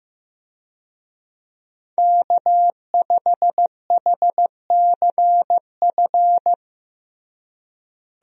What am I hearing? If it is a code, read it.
Morse code, heard as a keyed tone, about 15 wpm, 700 Hz. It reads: K5HCF
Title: USA Call Signs - 5 Characters - Repeated 3x - 15wpm